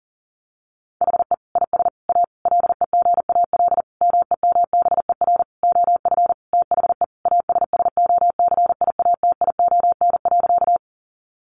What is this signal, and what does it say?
Morse code, heard as a keyed tone, about 40 wpm, 700 Hz. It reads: HE IS A LEGAL MEMBER OF THE ASSOCIATION.
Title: Sentences from Top 1200 Words 40wpm